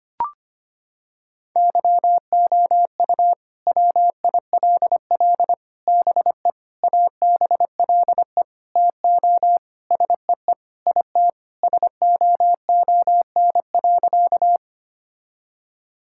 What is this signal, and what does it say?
Morse code, heard as a keyed tone, about 25 wpm, 700 Hz. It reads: YOU WILL BE ABLE TO SEE IT SOON.
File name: practice / 25wpm / Sentences with 4-Word Phrases - 25wpm